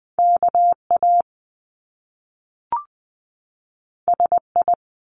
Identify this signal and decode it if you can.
Morse code, heard as a keyed tone, about 20 wpm, 700 Hz. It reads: KA SI
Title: Top Two Letter Combinations in Words 20wpm